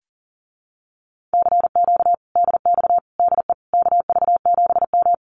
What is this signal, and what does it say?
Morse code, heard as a keyed tone, about 41 wpm, 700 Hz. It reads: CQ DX DE KV7M
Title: Common QSO Phrases Encore 40wpm